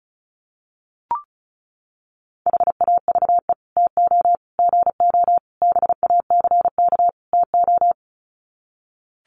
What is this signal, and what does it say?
Morse code, heard as a keyed tone, about 34 wpm, 700 Hz. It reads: HAVE TO GO BACK TO
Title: Sets of 5 Words Encore2 35wpm